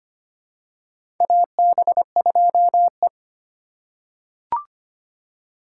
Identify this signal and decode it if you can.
Morse code, heard as a keyed tone, about 25 wpm, 700 Hz. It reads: AB2E
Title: USA Call Signs - 2x Character Spacing - 25wpm